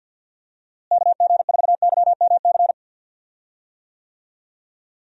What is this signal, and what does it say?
Morse code, heard as a keyed tone, about 52 wpm, 700 Hz. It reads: KG4YMC
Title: USA Call Signs - 6 Characters - Repeated 2x - 50wpm